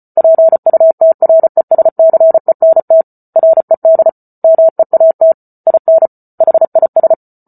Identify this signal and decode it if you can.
Morse code, heard as a keyed tone, about 34 wpm, 650 Hz. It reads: PUTRESCENT RED MEAT IN HIS